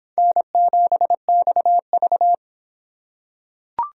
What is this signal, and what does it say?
Morse code, heard as a keyed tone, about 23 wpm, 700 Hz. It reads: N7XV